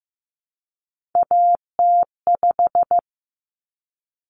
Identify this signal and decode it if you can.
Morse code, heard as a keyed tone, about 15 wpm, 700 Hz. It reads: AT5
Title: Common Cut Numbers - 3-Digit 15wpm